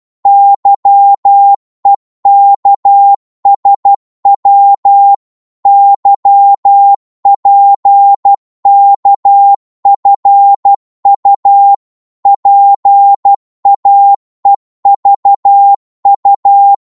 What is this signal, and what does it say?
Morse code, heard as a keyed tone, about 12 wpm, 800 Hz. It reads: YEKSW YPKFU PAEVU
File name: Morse / ltr-12wpm